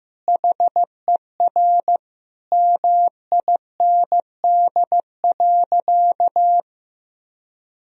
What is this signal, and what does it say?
Morse code, heard as a keyed tone, about 16 wpm, 700 Hz. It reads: HER MIND.
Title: Sentences from Top 1000 Words - Review Part 1 - 15wpm